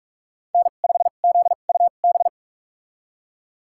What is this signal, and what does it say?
Morse code, heard as a keyed tone, about 46 wpm, spaced as for 33 wpm, 700 Hz. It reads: N5ZUB
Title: USA Call Signs - 5 Characters - 2x Character Spacing - 45wpm